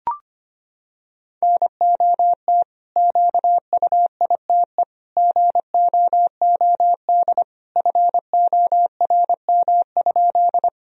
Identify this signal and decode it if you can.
Morse code, heard as a keyed tone, about 25 wpm, 700 Hz. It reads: NOT QUITE GOOD FORM?
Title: Sentences from Top 1000 Words - Review Part 1 - 25wpm